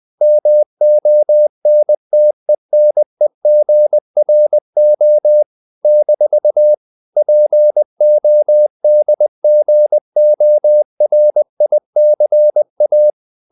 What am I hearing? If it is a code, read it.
Morse code, heard as a keyed tone, about 20 wpm, 600 Hz. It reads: MONTENEGRO - PODGORICA
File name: CW / Practice / NationalCapitals20